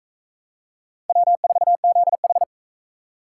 Morse code, heard as a keyed tone, about 44 wpm, 700 Hz. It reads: W4ZH